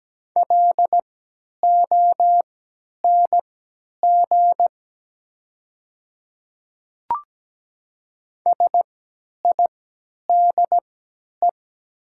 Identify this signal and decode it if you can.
Morse code, heard as a keyed tone, about 17 wpm, spaced as for 10 wpm, 700 Hz. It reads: LONG SIDE
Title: 3x Character Spacing - Top 100 Words - 17wpm